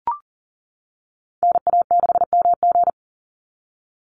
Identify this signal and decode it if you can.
Morse code, heard as a keyed tone, about 40 wpm, 700 Hz. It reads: NA6MG
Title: NA SOTA Activator Callsigns 40wpm